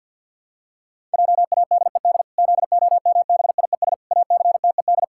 Morse code, heard as a keyed tone, about 50 wpm, 700 Hz. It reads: JADED ZOMBIES ACTED
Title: Pangram Challenge v2 - 50wpm